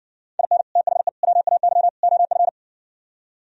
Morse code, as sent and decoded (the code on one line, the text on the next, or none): .. -. / - .... / .-- .- -.-- / --- ..-.
IN THE WAY OF